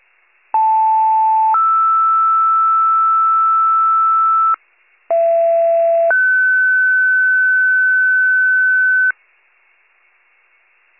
Two-tone page, heard as a one-second tone, then a three-second tone, about 880 then 1320 Hz, about 670 then 1500 Hz. It is heard from 0.5 to 9.1 s.